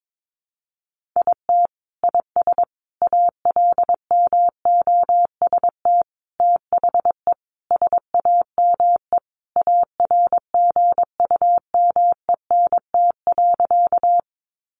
Morse code, heard as a keyed tone, about 22 wpm, 700 Hz. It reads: IT IS ALMOST THE SAME ARGUMENT.
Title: Sentences from Top 1400 Words 22wpm